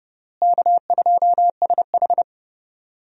Morse code, heard as a keyed tone, about 30 wpm, 700 Hz. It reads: K2SH
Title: USA Call Signs - 4 Characters - Repeated 3x - 30wpm